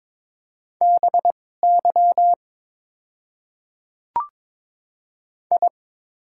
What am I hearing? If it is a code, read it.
Morse code, heard as a keyed tone, about 22 wpm, spaced as for 16 wpm, 700 Hz. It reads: BY I
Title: 2x Character Spacing - Top 100 Words - 22wpm